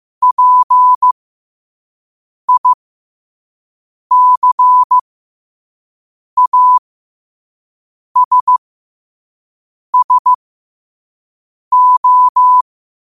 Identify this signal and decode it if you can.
Morse code, heard as a keyed tone, about 15 wpm, spaced as for 5 wpm, 1000 Hz. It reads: PICASSO
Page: New quotes every day in morse code at 5 Words per minute.